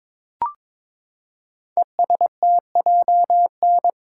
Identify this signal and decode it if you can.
Morse code, heard as a keyed tone, about 22 wpm, 700 Hz. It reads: ESTJN
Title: Code Groups - 5 Letter:Numbers - 22wpm